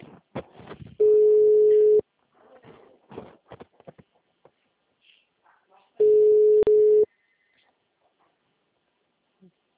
Ringback tone (Europe, 425 Hz).
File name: out- -2018- - - .38078